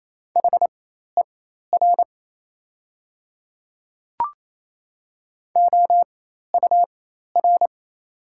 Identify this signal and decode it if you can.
Morse code, heard as a keyed tone, about 28 wpm, spaced as for 13 wpm, 690 Hz. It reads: HER OUR
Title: 4x Character Spacing - Top 200 Words - 28wpm